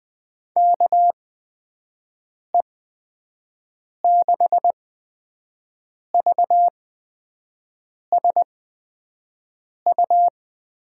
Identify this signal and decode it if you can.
Morse code, heard as a keyed tone, about 20 wpm, spaced as for 5 wpm, 700 Hz. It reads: KE6VSU
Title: USA Call Signs - 6 Characters - 8x Character Spacing - 20wpm